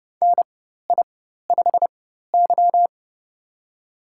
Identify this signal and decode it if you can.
Morse code, heard as a keyed tone, about 32 wpm, spaced as for 14 wpm, 700 Hz. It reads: NI5Y
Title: USA Call Signs - 4 Characters - 4x Character Spacing - 30wpm